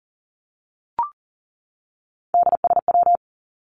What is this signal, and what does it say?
Morse code, heard as a keyed tone, about 40 wpm, 700 Hz. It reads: DSW
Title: Abbreviations 40wpm